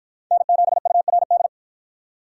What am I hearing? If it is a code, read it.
Morse code, heard as a keyed tone, about 48 wpm, 700 Hz. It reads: N7URD